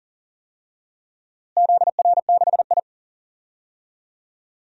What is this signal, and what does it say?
Morse code, heard as a keyed tone, about 39 wpm, 700 Hz. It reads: ZR6I
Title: DXCC-100 Call Signs - 4 Characters - Repeated 3x - 40wpm